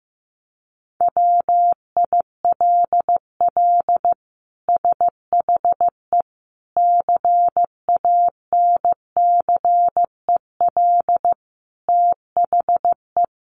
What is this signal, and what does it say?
Morse code, heard as a keyed tone, about 15 wpm, 700 Hz. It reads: WILL SHE CANCEL THE
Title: Sentences from Top 1200 Words 15wpm